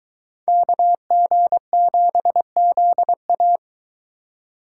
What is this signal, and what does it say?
Morse code, heard as a keyed tone, about 23 wpm, 700 Hz. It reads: KG7ZA